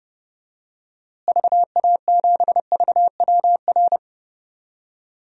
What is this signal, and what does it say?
Morse code, heard as a keyed tone, about 32 wpm, 700 Hz. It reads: VA7VWR
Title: DXCC-100 Call Signs - 6 Characters - Repeated 2x - 30wpm